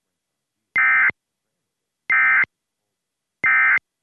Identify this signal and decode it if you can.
Emergency Alert System (SAME) tones, heard as harsh buzzing bursts. It is heard from 0.7 to 3.8 s.